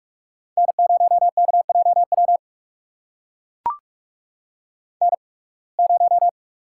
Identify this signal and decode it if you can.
Morse code, heard as a keyed tone, about 44 wpm, spaced as for 29 wpm, 700 Hz. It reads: N0KJW N0
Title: USA Call Signs - 5 Characters - 8x Character Spacing - 45wpm